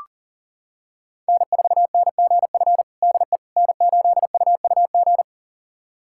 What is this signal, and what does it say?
Morse code, heard as a keyed tone, about 40 wpm, 700 Hz. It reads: N4NGF DE N8UUG
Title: USA Callsign DE Callsign - 5 Characters - 40wpm